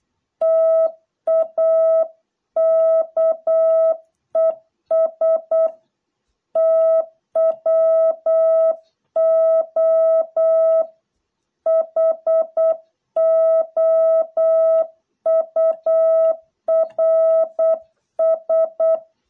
Morse code, heard as a keyed tone, about 8 wpm, 660 Hz. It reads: TAKES TWO HOURS